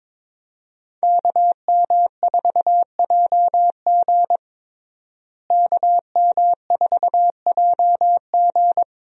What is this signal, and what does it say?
Morse code, heard as a keyed tone, about 22 wpm, 700 Hz. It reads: KM4JG KM4JG
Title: USA Call Signs - 5 Characters - Repeated 2x - 22wpm